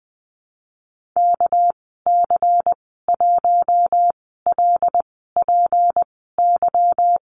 Morse code, heard as a keyed tone, about 20 wpm, spaced as for 15 wpm, 700 Hz. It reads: KC1LPY
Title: 231 Lesson 37g - Letter x - Call Signs 20wpm